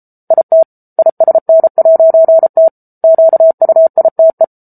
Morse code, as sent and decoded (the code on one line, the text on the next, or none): .. - / .. ... -. .----. - / --.- ..- .. - .
IT ISN'T QUITE